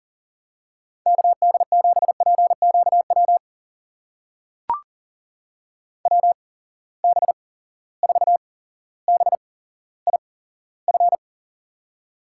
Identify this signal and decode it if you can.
Morse code, heard as a keyed tone, about 39 wpm, spaced as for 17 wpm, 700 Hz. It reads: KD7PQW WB4BIF
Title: USA Call Signs - 6 Characters - 8x Character Spacing - 40wpm